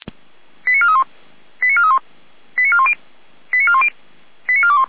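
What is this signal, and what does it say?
Five-tone selective call (ZVEI), heard as quick runs of five tones, tones of about 70 ms. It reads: 89431 89431 89410 89410 89421
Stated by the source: ZVEI five-tone call received by SDR.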